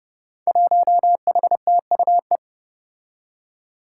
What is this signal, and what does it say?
Morse code, heard as a keyed tone, about 32 wpm, 700 Hz. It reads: 1HTUE
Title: Five Challenging Letter-Number Combinations 30wpm